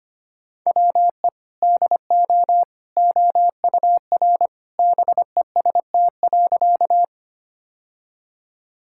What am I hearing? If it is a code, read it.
Morse code, heard as a keyed tone, about 25 wpm, 700 Hz. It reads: WE DO OUR BEST.
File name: practice / 25wpm / Sentences with 2-Word Phrases - 25wpm